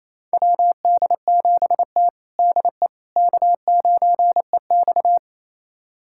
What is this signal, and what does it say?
Morse code, heard as a keyed tone, about 28 wpm, 690 Hz. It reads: WD7T DE K9EX
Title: USA Callsign DE Callsign - 4 Characters - 28wpm